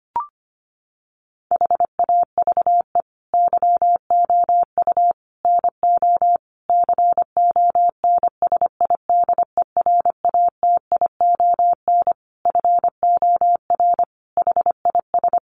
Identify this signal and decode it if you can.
Morse code, heard as a keyed tone, about 25 wpm, 700 Hz. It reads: HAVE YOU NO CONSIDERATION FOR HIS
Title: Sentences from Top 1500 Words 25wpm